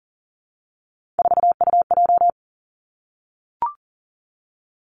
Morse code, heard as a keyed tone, about 40 wpm, 700 Hz. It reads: 4UJ
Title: Three Challenging Letter-Number Combinations 40wpm